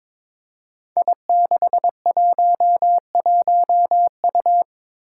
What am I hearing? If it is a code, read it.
Morse code, heard as a keyed tone, about 22 wpm, 700 Hz. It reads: I611U
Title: Five Challenging Letter-Number Combinations 22wpm